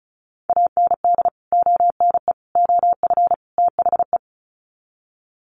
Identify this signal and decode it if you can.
Morse code, heard as a keyed tone, about 34 wpm, 700 Hz. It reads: AND ONE OF THE
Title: Sets of 4 Words 35wpm